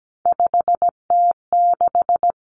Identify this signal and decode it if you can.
Morse code, heard as a keyed tone, about 17 wpm, 700 Hz. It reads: 5T6
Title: Common Cut Numbers - 3-Digit 17wpm